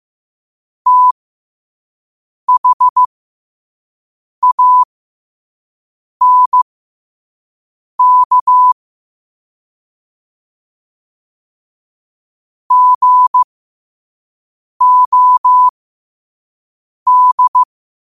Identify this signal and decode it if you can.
Morse code, heard as a keyed tone, about 15 wpm, spaced as for 5 wpm, 1000 Hz. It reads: THANK GOD
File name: qotdfiles / 5wpm